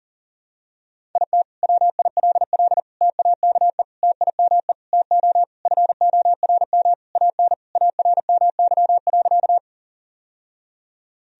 Morse code, heard as a keyed tone, about 40 wpm, 700 Hz. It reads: IT WILL TAKE TIME TO FORM AN ARMY.
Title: Sentences from Top 1000 Words - Review Part 1 - 40wpm